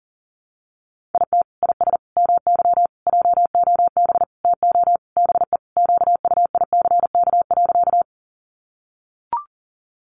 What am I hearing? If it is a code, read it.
Morse code, heard as a keyed tone, about 40 wpm, 700 Hz. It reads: IT IS MY JOB TO BE QUICK.
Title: Sentences from Top 600 Words 40wpm